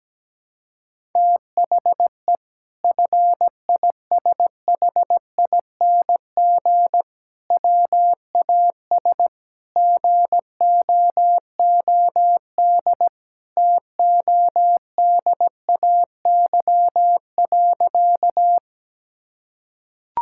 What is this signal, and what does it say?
Morse code, heard as a keyed tone, about 17 wpm, 700 Hz. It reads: THE FISHING WAS GOOD TODAY.
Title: Sentences from Top 1000 Words - Review Part 5 - 17wpm